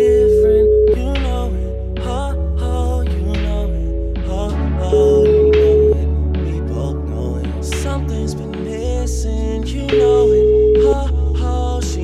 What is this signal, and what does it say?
Ringback tone (Europe, 425 Hz).